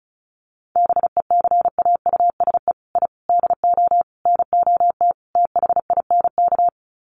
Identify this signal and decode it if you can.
Morse code, heard as a keyed tone, about 34 wpm, 700 Hz. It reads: BECAUSE I DO NOT THINK
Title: Sets of 5 Words Encore 35wpm